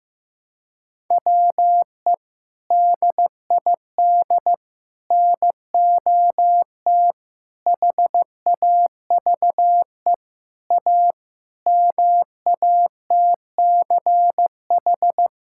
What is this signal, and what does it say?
Morse code, heard as a keyed tone, about 15 wpm, 700 Hz. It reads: WE DID NOT HAVE A MATCH
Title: Sentences from Top 1600 Words 15wpm